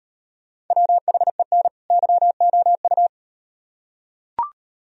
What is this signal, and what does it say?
Morse code, heard as a keyed tone, about 36 wpm, 700 Hz. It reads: WHEN YOU